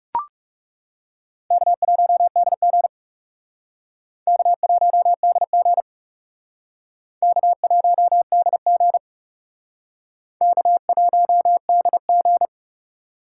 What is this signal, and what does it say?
Morse code, heard as a keyed tone, about 37 wpm, 700 Hz. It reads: K1DG K1DG K1DG K1DG
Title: Speed Racing - USA 1x2 Call Signs 30wpm